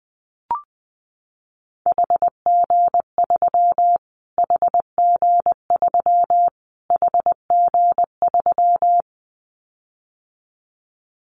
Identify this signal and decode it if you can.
Morse code, heard as a keyed tone, about 20 wpm, 700 Hz. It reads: HG3 HG3 HG3
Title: Callsign Prefixes for Top 25 DXCC - Repeated 3x - 20wpm